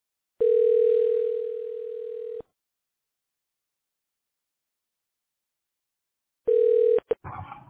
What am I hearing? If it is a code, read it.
Ringback tone (North America), 440 + 480 Hz.